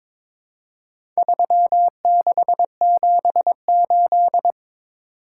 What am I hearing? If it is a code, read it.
Morse code, heard as a keyed tone, about 23 wpm, 700 Hz. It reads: 3678
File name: practice / 22wpm / Four Letter-Number Combinations - 22wpm